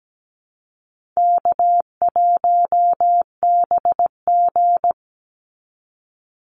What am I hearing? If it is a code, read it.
Morse code, heard as a keyed tone, about 17 wpm, 700 Hz. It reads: K1BG
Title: Easy Call Signs - 17wpm